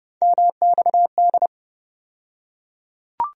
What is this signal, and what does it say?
Morse code, heard as a keyed tone, about 30 wpm, 700 Hz. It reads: MXD
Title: USA Call Sign Postfixes - 3 Letters 30wpm